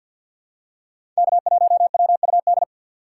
Morse code, heard as a keyed tone, about 47 wpm, 700 Hz. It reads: K1WUD
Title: USA Call Signs - 2x Character Spacing - 50wpm